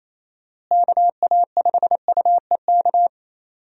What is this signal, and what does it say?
Morse code, heard as a keyed tone, about 28 wpm, 700 Hz. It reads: KA5UEK
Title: USA Call Signs - 6 Characters - Repeated 3x - 28wpm